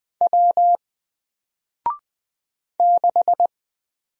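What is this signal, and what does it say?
Morse code, heard as a keyed tone, about 20 wpm, 700 Hz. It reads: W 6
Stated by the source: Single letter and number Morse drill at 20 wpm, from N4LCD.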